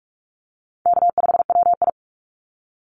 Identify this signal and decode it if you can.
Morse code, heard as a keyed tone, about 46 wpm, 690 Hz. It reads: K5WI
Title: NA SOTA Chaser Callsigns 45wpm